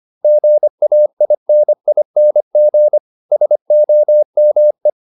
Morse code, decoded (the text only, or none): GAINING SOME